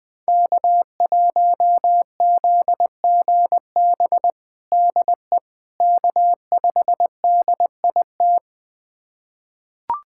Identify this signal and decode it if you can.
Morse code, heard as a keyed tone, about 20 wpm, 700 Hz. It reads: K1ZGB DE K5DIT
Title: USA Callsign DE Callsign - 5 Characters - 20wpm